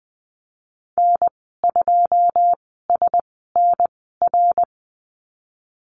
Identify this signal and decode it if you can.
Morse code, heard as a keyed tone, about 19 wpm, spaced as for 14 wpm, 700 Hz. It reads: N2SNR
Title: 088 Lesson 15g - Number 5 - Call Signs 20wpm